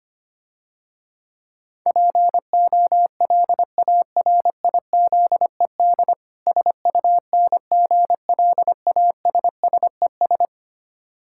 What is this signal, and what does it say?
Morse code, heard as a keyed tone, about 25 wpm, 700 Hz. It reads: POLARIZED SUNGLASSES
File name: practice / 25wpm / Difficult Words - Speed-Racing with Hints - 25wpm